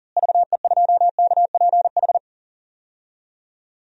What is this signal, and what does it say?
Morse code, heard as a keyed tone, about 40 wpm, 700 Hz. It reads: VE2KPH
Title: DXCC-100 Call Signs - 6 Characters - Repeated 2x - 40wpm